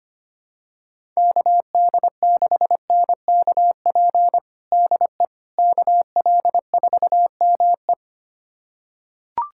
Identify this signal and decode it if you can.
Morse code, heard as a keyed tone, about 25 wpm, 700 Hz. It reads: KD6NKP DE KL4ME
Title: USA Callsign DE Callsign 25wpm